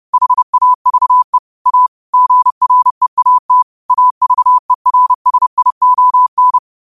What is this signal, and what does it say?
Morse code, heard as a keyed tone, about 30 wpm, 1000 Hz. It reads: HAVE A GREAT AVERSION